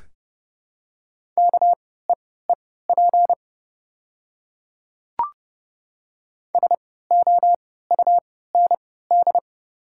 Morse code, heard as a keyed tone, about 30 wpm, spaced as for 17 wpm, 690 Hz. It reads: KEEP SOUND
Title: 3x Character Spacing - Top 300 Words - 30wpm